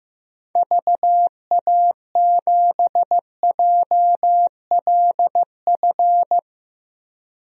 Morse code, heard as a keyed tone, about 16 wpm, 700 Hz. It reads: VA7JLF
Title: DXCC-100 Call Signs - 6 Characters - Repeated 2x - 15wpm